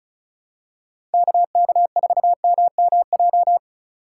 Morse code, heard as a keyed tone, about 36 wpm, 700 Hz. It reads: KK4MMJ
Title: USA Call Signs - 6 Characters - 4x Character Spacing - 35wpm